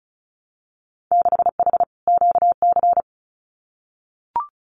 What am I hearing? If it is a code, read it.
Morse code, heard as a keyed tone, about 34 wpm, 700 Hz. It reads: 6H QC
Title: Winter Field Day Exchanges Encore 35wpm